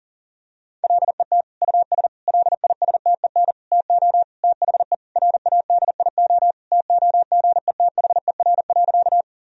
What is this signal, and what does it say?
Morse code, heard as a keyed tone, about 40 wpm, 700 Hz. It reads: LET US LISTEN TO THE RADIO TOGETHER.
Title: Sentences from Top 1000 Words - Review Part 3 - 40wpm